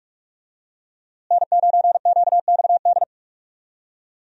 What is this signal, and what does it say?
Morse code, heard as a keyed tone, about 42 wpm, 700 Hz. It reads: N9QXD
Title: USA Call Signs - 5 Characters - Repeated 2x - 45wpm